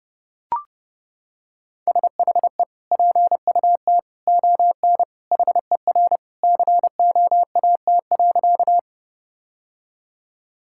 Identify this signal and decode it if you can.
Morse code, heard as a keyed tone, about 30 wpm, 700 Hz. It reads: SHE PUT ON HER COAT.
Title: Sentences from Top 1600 Words 30wpm